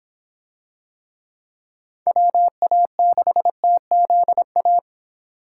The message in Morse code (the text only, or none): WA6TZA